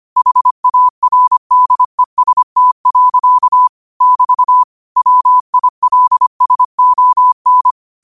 Morse code, heard as a keyed tone, about 25 wpm, 1000 Hz. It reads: HARDEST. <BT> WILSON
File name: qotdfiles / 25wpm